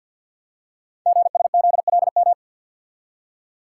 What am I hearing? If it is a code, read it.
Morse code, heard as a keyed tone, about 52 wpm, 700 Hz. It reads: GSZLM
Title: Code Groups - 5 Letters - 50wpm